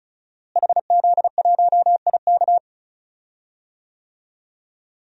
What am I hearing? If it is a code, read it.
Morse code, heard as a keyed tone, about 36 wpm, 700 Hz. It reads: HZ1IK
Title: DXCC-100 Call Signs - 5 Characters - Repeated 3x - 35wpm